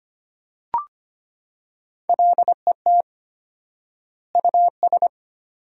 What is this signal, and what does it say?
Morse code, heard as a keyed tone, about 25 wpm, 700 Hz. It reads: LET US
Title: 4x Word-Spacing - Sentences from Top 400 Words - 25wpm